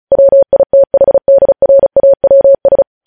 Morse code, heard as a keyed tone, about 36 wpm, 550 Hz. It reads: WITHDRAWS